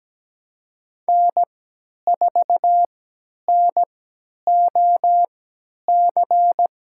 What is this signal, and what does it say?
Morse code, heard as a keyed tone, about 17 wpm, spaced as for 10 wpm, 700 Hz. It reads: N4NOC
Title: USA Call Signs - 5 Characters - 3x Character Spacing - 17wpm